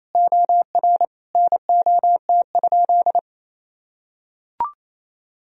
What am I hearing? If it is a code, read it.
Morse code, heard as a keyed tone, about 28 wpm, 700 Hz. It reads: OR NOT?
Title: Sentences from Top 1000 Words - Review Part 3 - 28wpm